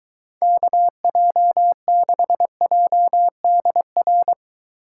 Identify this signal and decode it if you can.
Morse code, heard as a keyed tone, about 23 wpm, 700 Hz. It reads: KJ6JDR